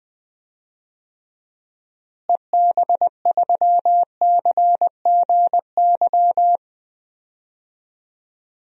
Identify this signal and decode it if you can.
Morse code, heard as a keyed tone, about 20 wpm, 700 Hz. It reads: EB3CGY
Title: DXCC-100 Call Signs - 6 Characters - Repeated 3x - 20wpm